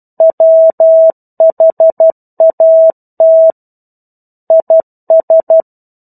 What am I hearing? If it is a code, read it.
Morse code, heard as a keyed tone, about 12 wpm, 650 Hz. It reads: WHAT IS
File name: 01-Chapter-01 12 WPM